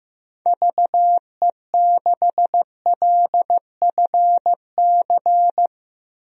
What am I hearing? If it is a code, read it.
Morse code, heard as a keyed tone, about 15 wpm, 700 Hz. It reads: VE6LFC
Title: DXCC-100 Call Signs - 6 Characters - 15wpm